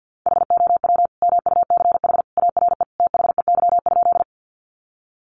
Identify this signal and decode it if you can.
Morse code, heard as a keyed tone, about 50 wpm, 700 Hz. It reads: HOW MUCH ARE THEY?
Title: Sentences from Top 200 Words 50wpm